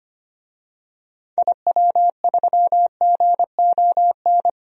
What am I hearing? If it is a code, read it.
Morse code, heard as a keyed tone, about 25 wpm, 700 Hz. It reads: IW3GON
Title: DXCC-100 Call Signs - 6 Characters - Repeated 3x - 25wpm